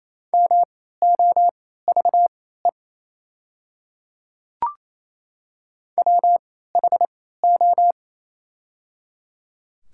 Morse code, heard as a keyed tone, about 28 wpm, spaced as for 16 wpm, 700 Hz. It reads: MOVE WHO
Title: 3x Character Spacing - Top 200 Words - 28wpm